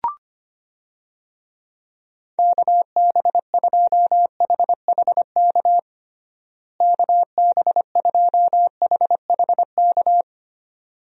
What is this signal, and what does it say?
Morse code, heard as a keyed tone, about 25 wpm, 700 Hz. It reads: KB2HHK KB2HHK
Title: USA Call Signs - 6 Characters - Repeated 2x - 25wpm